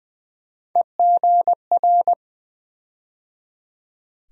Morse code, heard as a keyed tone, about 20 wpm, 700 Hz. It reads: EGR
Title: USA Call Sign Postfixes - 3 Letters 20wpm